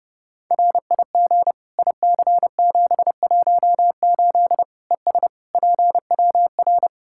Morse code, heard as a keyed tone, about 30 wpm, 700 Hz. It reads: RIG IC718 ES PWR
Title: Practice QSO - Exchange 2 - Info - 30wpm